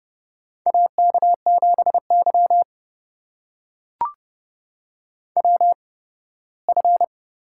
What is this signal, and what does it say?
Morse code, heard as a keyed tone, about 30 wpm, spaced as for 18 wpm, 700 Hz. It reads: AK7Y WF
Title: USA Call Signs - 4 Characters - 8x Character Spacing - 30wpm